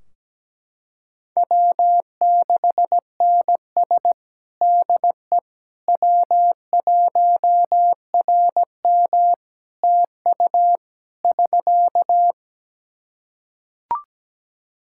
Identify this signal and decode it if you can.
Morse code, heard as a keyed tone, about 17 wpm, 700 Hz. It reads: W6NS DE W1RM TU <SK>
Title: K1USN SST QSO Practice 17wpm